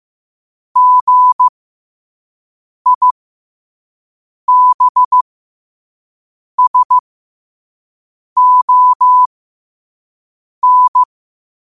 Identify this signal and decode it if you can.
Morse code, heard as a keyed tone, about 15 wpm, spaced as for 5 wpm, 1000 Hz. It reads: GIBSON
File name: 5wpm